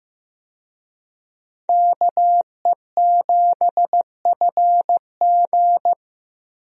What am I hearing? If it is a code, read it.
Morse code, heard as a keyed tone, about 16 wpm, 700 Hz. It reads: KE7FG
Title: USA Call Signs - 5 Characters - Repeated 2x - 15wpm